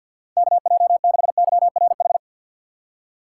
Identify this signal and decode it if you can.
Morse code, heard as a keyed tone, about 53 wpm, 700 Hz. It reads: KJ6YRH
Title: USA Call Signs - 6 Characters - 4x Character Spacing - 50wpm